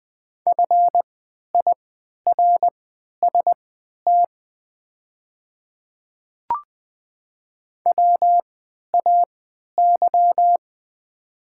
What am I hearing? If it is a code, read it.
Morse code, heard as a keyed tone, about 20 wpm, spaced as for 11 wpm, 690 Hz. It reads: FIRST WAY
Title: 3x Character Spacing - Top 300 Words - 20wpm